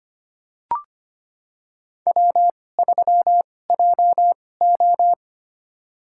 Morse code, heard as a keyed tone, about 25 wpm, spaced as for 18 wpm, 700 Hz. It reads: W3JO
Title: USA Call Signs - 4 Characters - 2x Character Spacing - 25wpm